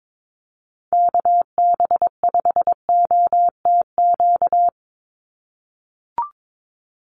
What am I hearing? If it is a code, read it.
Morse code, heard as a keyed tone, about 22 wpm, 700 Hz. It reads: KB5OTQ
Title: USA Call Signs 22wpm